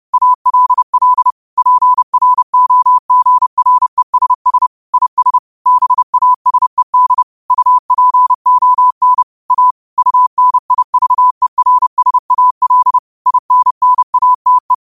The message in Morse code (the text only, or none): ALL PROGRESS IS BASED UPON A UNIVERSAL INNATE